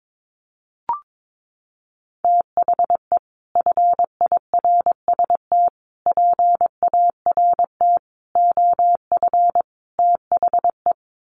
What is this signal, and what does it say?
Morse code, heard as a keyed tone, about 22 wpm, 700 Hz. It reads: THE FIRST PART OF THE
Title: Sets of 5 Words Encore 22wpm